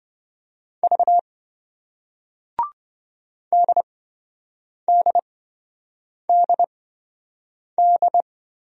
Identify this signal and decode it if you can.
Morse code, heard as a keyed tone, about 28 wpm, spaced as for 7 wpm, 700 Hz. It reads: V DDDD